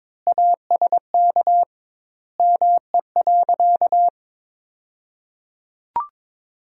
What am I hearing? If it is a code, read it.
Morse code, heard as a keyed tone, about 21 wpm, 700 Hz. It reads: ASK ME.
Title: 2x Word-Spacing - Sentences from Top 500 Words - 22wpm